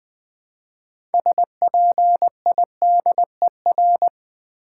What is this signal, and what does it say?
Morse code, heard as a keyed tone, about 20 wpm, 700 Hz. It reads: SPIDER
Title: 6 Letter Words - First 3 Letters Given - 20wpm